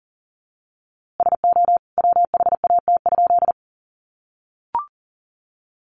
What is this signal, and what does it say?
Morse code, heard as a keyed tone, about 40 wpm, 700 Hz. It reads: SO WHAT?
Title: A Sound Of Thunder 40wpm